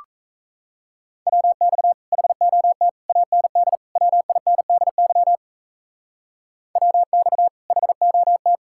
Morse code, heard as a keyed tone, about 34 wpm, 700 Hz. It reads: WX HOT AND WINDY WX HOT
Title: Speed-Racing Common QSO Phrases Encore - 28wpm